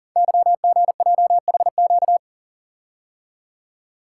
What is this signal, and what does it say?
Morse code, heard as a keyed tone, about 40 wpm, 700 Hz. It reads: YGJHQ
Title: Code Groups - 5 Letters - 40wpm